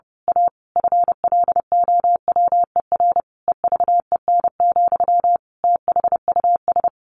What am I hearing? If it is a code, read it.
Morse code, heard as a keyed tone, about 30 wpm, 700 Hz. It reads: A FLOWER EVEN, THUS
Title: A Sound Of Thunder 30wpm